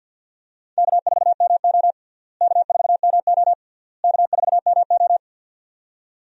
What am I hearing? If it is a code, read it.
Morse code, heard as a keyed tone, about 52 wpm, 700 Hz. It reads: K4MO K4MO K4MO
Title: USA Call Signs - 4 Characters - Repeated 3x - 50wpm